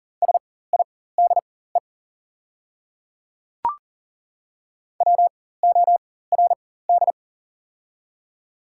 Morse code, heard as a keyed tone, about 39 wpm, spaced as for 18 wpm, 700 Hz. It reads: SIDE WORD
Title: 4x Character Spacing - Top 100 Words - 40wpm